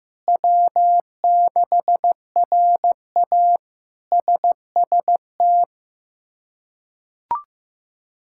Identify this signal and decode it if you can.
Morse code, heard as a keyed tone, about 15 wpm, 700 Hz. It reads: W6RA SST
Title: K1USN SST QSO Practice - 15wpm